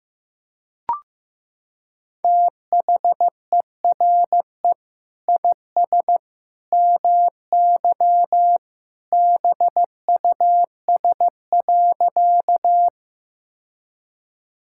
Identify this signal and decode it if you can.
Morse code, heard as a keyed tone, about 15 wpm, 700 Hz. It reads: THERE IS MY BUS.
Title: Sentences from Top 1000 Words - Review Part 6 - 15wpm